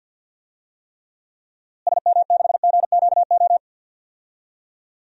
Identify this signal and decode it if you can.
Morse code, heard as a keyed tone, about 52 wpm, 700 Hz. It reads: SM6GQO
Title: DXCC-100 Call Signs - 6 Characters - Repeated 2x - 50wpm